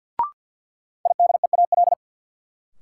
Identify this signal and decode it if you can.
Morse code, heard as a keyed tone, about 53 wpm, 700 Hz. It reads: IDEAL
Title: Sentences from Top 1400 Words - New Word Drill Alt - 50wpm